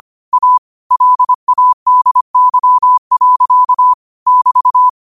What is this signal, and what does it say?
Morse code, heard as a keyed tone, about 25 wpm, 1000 Hz. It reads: A LADY. <BT>
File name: qotdfiles / 25wpm